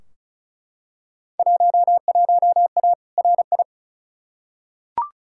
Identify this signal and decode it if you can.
Morse code, heard as a keyed tone, about 36 wpm, 700 Hz. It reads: 11A RI
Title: Field Day Exchanges Encore - 35wpm